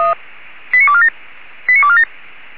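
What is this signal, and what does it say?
Five-tone selective call (ZVEI), heard as quick runs of five tones, tones of about 70 ms. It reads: 89247 89247
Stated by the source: ZVEI five-tone selective call received by SDR.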